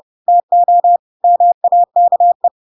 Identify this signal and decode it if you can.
Morse code, heard as a keyed tone, about 30 wpm, 700 Hz. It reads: TO MAKE
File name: Pinocchio / Pinocchio ch3 30 30 ik1hge